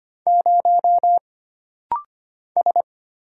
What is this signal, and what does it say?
Morse code, heard as a keyed tone, about 25 wpm, 700 Hz. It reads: 0 S
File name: Single Character (Letter:Number:Punctuation:Prosign) - Rapid-Fire - 25wpm